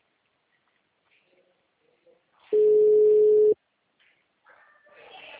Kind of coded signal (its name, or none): telephone call progress tone